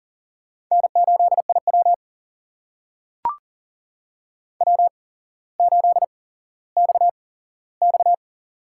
Morse code, heard as a keyed tone, about 40 wpm, spaced as for 17 wpm, 690 Hz. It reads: N8IW W8XX
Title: USA Call Signs - 8x Character Spacing - 40wpm